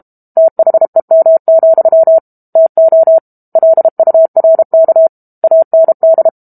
Morse code, heard as a keyed tone, about 32 wpm, 650 Hz. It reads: THEM, TO LURK AND
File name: Book Two / 06-Chapter 6 - 32 WPM